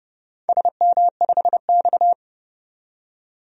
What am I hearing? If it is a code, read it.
Morse code, heard as a keyed tone, about 29 wpm, 700 Hz. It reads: SM5X
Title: DXCC-100 Call Signs - 4 Characters - 30wpm